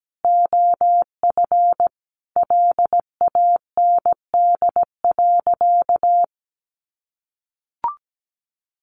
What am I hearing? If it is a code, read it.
Morse code, heard as a keyed tone, about 17 wpm, 700 Hz. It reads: OF LAND.
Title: Sentences from Top 1300 Words 17wpm